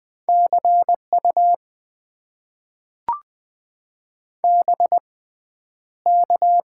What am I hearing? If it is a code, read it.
Morse code, heard as a keyed tone, about 20 wpm, spaced as for 10 wpm, 700 Hz. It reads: CU BK
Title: Top 90 QSO Elements - 6x Character Spacing - 20wpm